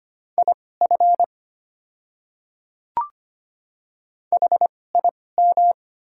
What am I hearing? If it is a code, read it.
Morse code, heard as a keyed tone, about 25 wpm, spaced as for 18 wpm, 700 Hz. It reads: IF HIM
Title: 2x Character Spacing - Top 100 Words - 25wpm